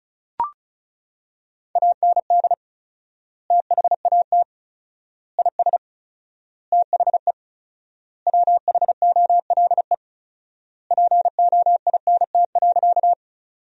Morse code, heard as a keyed tone, about 36 wpm, 700 Hz. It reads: AND THAT IS THE WHOLE POINT.
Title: 4x Word-Spacing - Sentences from Top 400 Words - 35wpm